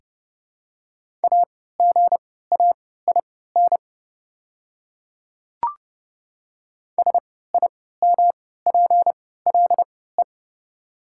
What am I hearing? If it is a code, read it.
Morse code, heard as a keyed tone, about 31 wpm, spaced as for 17 wpm, 700 Hz. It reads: AGAIN SIMPLE
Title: 3x Character Spacing - Top 500 Words - 30wpm